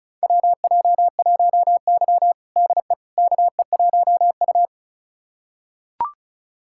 Morse code, heard as a keyed tone, about 36 wpm, 700 Hz. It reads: WJ1Y DE KE1U
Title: USA Callsign DE Callsign - 4 Characters - 35wpm